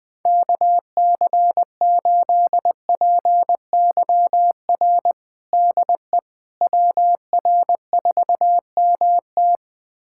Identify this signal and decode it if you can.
Morse code, heard as a keyed tone, about 20 wpm, 700 Hz. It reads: KC8PYR DE WR4MT
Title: USA Callsign DE Callsign 20wpm